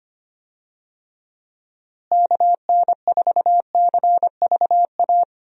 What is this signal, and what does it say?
Morse code, heard as a keyed tone, about 25 wpm, 700 Hz. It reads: KN4CVA
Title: USA Call Signs - 6 Characters - Repeated 2x - 25wpm